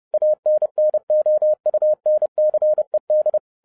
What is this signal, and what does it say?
Morse code, heard as a keyed tone, about 30 wpm, 610 Hz. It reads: ANNOUNCED